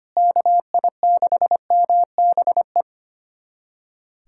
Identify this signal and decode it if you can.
Morse code, heard as a keyed tone, about 25 wpm, 700 Hz. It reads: KI6MBE
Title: USA Callsign DE Callsign - 6 Characters - 25wpm